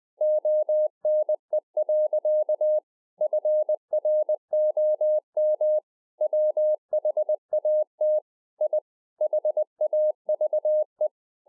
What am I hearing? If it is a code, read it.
Morse code, heard as a keyed tone, about 20 wpm, 610 Hz. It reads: ONE. FROM WHAT I HAVE
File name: 20WPM